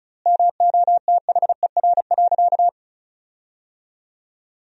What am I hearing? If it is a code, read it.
Morse code, heard as a keyed tone, about 36 wpm, 700 Hz. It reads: MOTHER.
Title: 3x Word-Spacing - Sentences from Top 400 Words - 35wpm